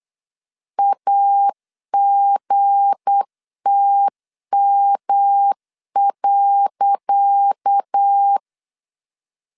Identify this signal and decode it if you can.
Morse code, heard as a keyed tone, about 8 wpm, 800 Hz. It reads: AGTM.
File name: LEZIONI / 10 SIGNS / 10C DOT 8wpm 3649